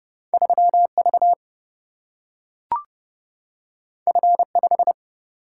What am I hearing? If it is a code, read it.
Morse code, heard as a keyed tone, about 31 wpm, 700 Hz. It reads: 3 V F 5
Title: Single-Letter Single-Number 30wpm